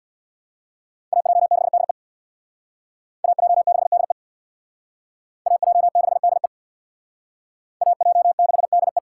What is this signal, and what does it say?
Morse code, heard as a keyed tone, about 69 wpm, 700 Hz. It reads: AJ6DE AJ6DE AJ6DE AJ6DE